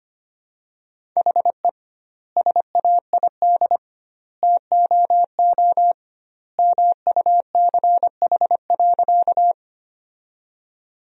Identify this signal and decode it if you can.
Morse code, heard as a keyed tone, about 25 wpm, 700 Hz. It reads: HE SAID TOO MUCH.
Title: 2x Word-Spacing - Sentences from Top 200 Words - 25wpm